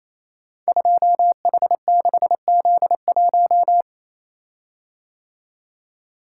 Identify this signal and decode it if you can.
Morse code, heard as a keyed tone, about 28 wpm, 700 Hz. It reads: 2H6Z1
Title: Code Groups - 5 Letter:Numbers - 28wpm